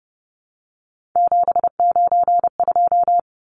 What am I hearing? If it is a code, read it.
Morse code, heard as a keyed tone, about 30 wpm, 700 Hz. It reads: 792
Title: Numbers - 3-Digit 30wpm